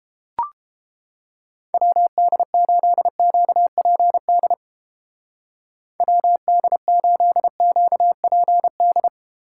Morse code, heard as a keyed tone, about 31 wpm, 700 Hz. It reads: WD8QPD WD8QPD